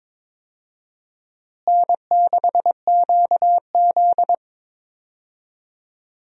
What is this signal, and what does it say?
Morse code, heard as a keyed tone, about 21 wpm, 700 Hz. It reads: N6QZ
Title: USA Call Signs - 4 Characters - Repeated 2x - 22wpm